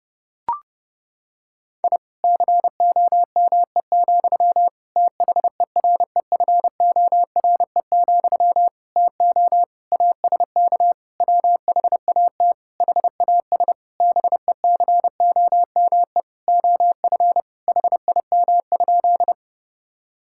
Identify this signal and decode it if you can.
Morse code, heard as a keyed tone, about 32 wpm, 700 Hz. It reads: I COME, THEREFORE, TO ASK WHAT HAS BECOME OF HIM?
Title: Sentences from Top 1000 Words - Review Part 5 - 30wpm